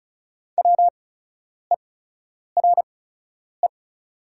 Morse code, heard as a keyed tone, about 36 wpm, spaced as for 10 wpm, 690 Hz. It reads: WERE
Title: 8x Character Spacing - Top 100 Words - 35wpm